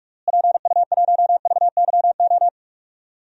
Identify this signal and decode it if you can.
Morse code, heard as a keyed tone, about 44 wpm, 700 Hz. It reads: PU1VYO
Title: DXCC-100 Call Signs - 6 Characters - Repeated 2x - 45wpm